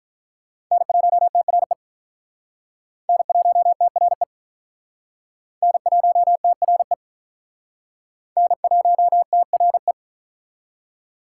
Morse code, heard as a keyed tone, about 48 wpm, 700 Hz. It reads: N1TRE N1TRE N1TRE N1TRE